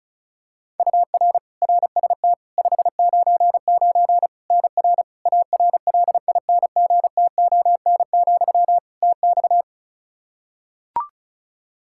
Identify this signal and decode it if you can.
Morse code, heard as a keyed tone, about 36 wpm, 700 Hz. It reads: UR RST 599 NR ARLINGTON, TX
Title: Practice QSO - Exchange 1 - Beginning - 35wpm